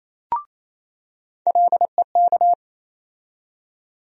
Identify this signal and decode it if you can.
Morse code, heard as a keyed tone, about 28 wpm, 700 Hz. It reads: LEK
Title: USA Call Sign Postfixes - 3 Letters 28wpm